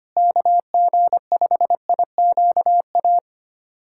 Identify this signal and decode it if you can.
Morse code, heard as a keyed tone, about 25 wpm, 700 Hz. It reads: KG5IQA
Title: USA Callsign DE Callsign - 6 Characters - 25wpm